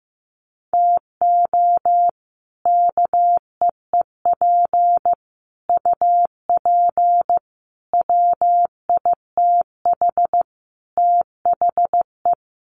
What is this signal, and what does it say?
Morse code, heard as a keyed tone, about 15 wpm, 700 Hz. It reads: TO KEEP UP WITH THE
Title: Sets of 5 Words Encore 15wpm